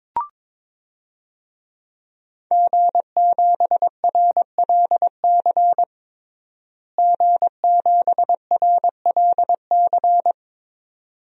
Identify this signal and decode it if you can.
Morse code, heard as a keyed tone, about 21 wpm, 700 Hz. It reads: G7RLC G7RLC
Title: DXCC-100 Call Signs - 5 Characters - Repeated 2x - 22wpm